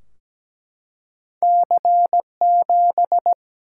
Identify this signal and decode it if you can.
Morse code, heard as a keyed tone, about 17 wpm, 700 Hz. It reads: C7
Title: Single-Letter Single-Number 17wpm